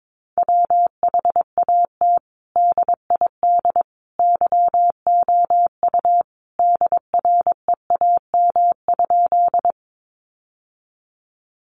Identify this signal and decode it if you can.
Morse code, heard as a keyed tone, about 22 wpm, 700 Hz. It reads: WHAT DID YOU DREAM?
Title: Sentences from Top 1500 Words 22wpm